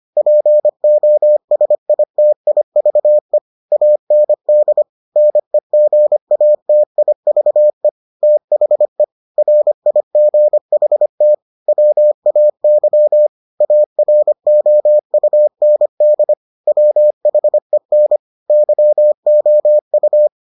Morse code, heard as a keyed tone, about 25 wpm, 600 Hz. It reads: POSITIVE AND NEGATIVE THE RIGHT WAY AROUND WHEN YOU